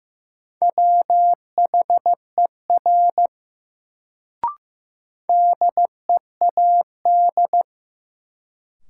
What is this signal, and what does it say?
Morse code, heard as a keyed tone, about 15 wpm, 700 Hz. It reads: WHER DEAD
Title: Sentences from Top 1500 Words - New Word Drill - 15wpm